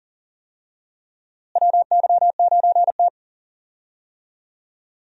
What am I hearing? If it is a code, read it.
Morse code, heard as a keyed tone, about 40 wpm, 700 Hz. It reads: WY9T
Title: USA Call Signs - 4 Characters - Repeated 3x - 40wpm